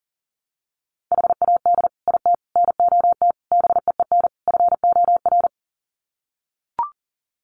Morse code, heard as a keyed tone, about 40 wpm, 700 Hz. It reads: HAD IT NOT BEEN FOR
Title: Sets of 5 Words Encore2 40wpm